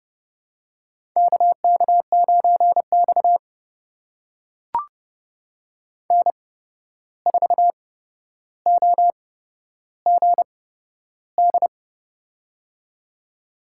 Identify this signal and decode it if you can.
Morse code, heard as a keyed tone, about 30 wpm, spaced as for 12 wpm, 700 Hz. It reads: KK9X N4OGD
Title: USA Call Signs - 8x Character Spacing - 30wpm